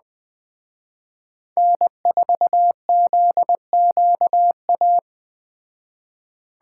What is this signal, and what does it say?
Morse code, heard as a keyed tone, about 20 wpm, 700 Hz. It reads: N4ZQA